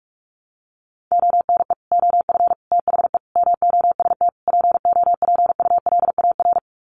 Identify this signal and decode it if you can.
Morse code, heard as a keyed tone, about 44 wpm, 700 Hz. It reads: ONE OF THE MOST POPULAR
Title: Sets of 5 Words 45wpm